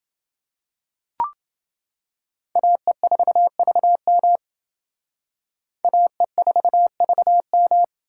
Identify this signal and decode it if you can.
Morse code, heard as a keyed tone, about 28 wpm, 700 Hz. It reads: AE4VM AE4VM